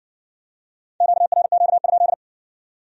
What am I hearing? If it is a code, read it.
Morse code, heard as a keyed tone, about 66 wpm, 700 Hz. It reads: QRQ?